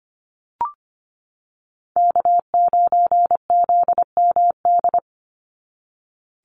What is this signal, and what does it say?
Morse code, heard as a keyed tone, about 25 wpm, 700 Hz. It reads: K9ZMD
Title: NA SOTA Chaser Callsigns 25wpm